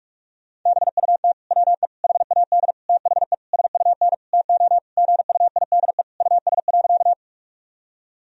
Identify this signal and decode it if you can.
Morse code, heard as a keyed tone, about 42 wpm, 700 Hz. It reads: BUT WE HAD THE SUN TO GUIDE US.
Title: Sentences from Top 1000 Words - Review Part 3 - 45wpm